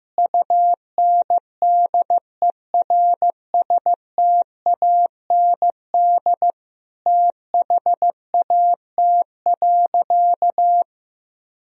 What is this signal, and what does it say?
Morse code, heard as a keyed tone, about 15 wpm, 700 Hz. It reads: UNDERSTAND THAT.
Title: Sentences from Top 1000 Words - Review Part 1 - 15wpm